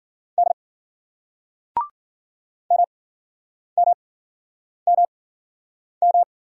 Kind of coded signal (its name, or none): Morse code